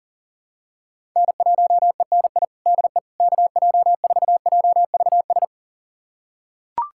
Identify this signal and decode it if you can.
Morse code, heard as a keyed tone, about 40 wpm, 700 Hz. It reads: N1ENI DE KJ4JVS
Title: USA Callsign DE Callsign 40wpm